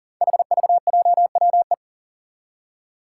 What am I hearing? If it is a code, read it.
Morse code, heard as a keyed tone, about 38 wpm, 700 Hz. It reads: HVJWE
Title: Five Challenging Letter-Number Combinations 40wpm